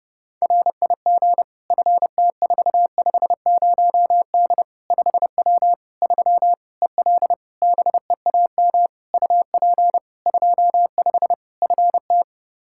Morse code, heard as a keyed tone, about 32 wpm, 700 Hz. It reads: RIG FT450D 5W 3 EL BEAM UP 25 FT
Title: Practice QSO - Exchange 2 - Info - 30wpm